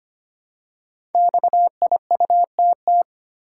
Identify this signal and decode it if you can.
Morse code, heard as a keyed tone, about 25 wpm, 700 Hz. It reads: XIUTT
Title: Code Groups - 5 Letters - 25wpm